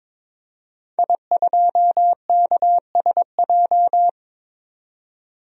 Morse code, heard as a keyed tone, about 22 wpm, 700 Hz. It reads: I2KSJ
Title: DXCC-100 Call Signs - 5 Characters - 22wpm